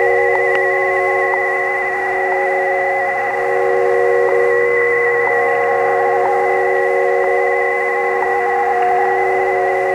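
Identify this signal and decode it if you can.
Two-tone teleprinter signal, about 80 Hz shift.
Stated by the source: Shortwave reception by amateur WA4CJF on a Kenwood TS-520, recorded on a Fostex X-15 four-track cassette recorder.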